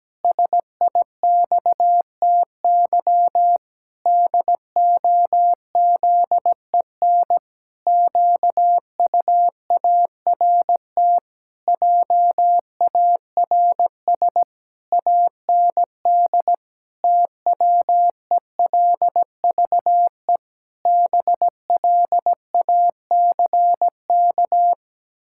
Morse code, heard as a keyed tone, about 17 wpm, 700 Hz. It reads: SIXTY DOZEN QUART JARS AND TWELVE BLACK
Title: Pangram Challenge v2 - 17wpm